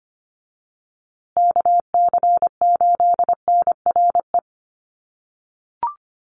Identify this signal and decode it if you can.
Morse code, heard as a keyed tone, about 25 wpm, 700 Hz. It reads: KC8NRE
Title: USA Call Signs Repeated 3x 25wpm